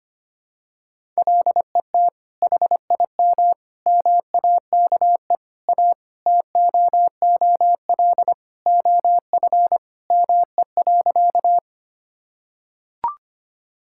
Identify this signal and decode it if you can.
Morse code, heard as a keyed tone, about 25 wpm, 700 Hz. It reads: LET HIM MAKE A TOOL OF ME.
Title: Sentences from Top 1000 Words - Review Part 6 - 25wpm